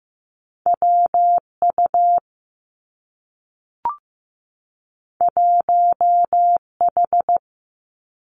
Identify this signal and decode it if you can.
Morse code, heard as a keyed tone, about 15 wpm, 700 Hz. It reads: WU 1H
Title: Two Challenging Letter-Number Combinations 15wpm